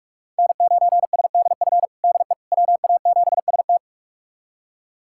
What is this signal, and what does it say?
Morse code, heard as a keyed tone, about 44 wpm, 700 Hz. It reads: N9SDF DE WA7ST
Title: USA Callsign DE Callsign - 5 Characters - 45wpm